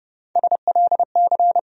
Morse code, heard as a keyed tone, about 30 wpm, 700 Hz. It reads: SLC
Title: USA Call Sign Postfixes - 3 Letters 30wpm